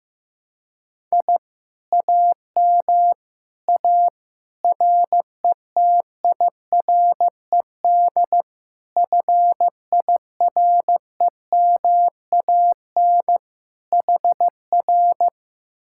Morse code, heard as a keyed tone, about 15 wpm, 700 Hz. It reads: I AM A RETIRED FIREMAN HR
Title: Speed-Racing Common QSO Phrases Encore - 15wpm